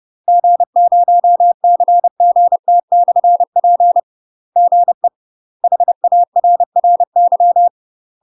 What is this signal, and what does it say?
Morse code, heard as a keyed tone, about 30 wpm, 700 Hz. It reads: G0CGT/P GE HARRY
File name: UK MASTER QSOs TRAINING / 30WPM / UK MASTER QSOs 0032